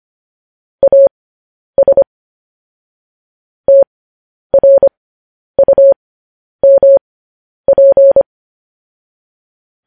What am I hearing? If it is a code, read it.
Morse code, heard as a keyed tone, about 25 wpm, spaced as for 10 wpm, 550 Hz. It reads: AS TRUMP
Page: news headlines in morse code at 10 wpm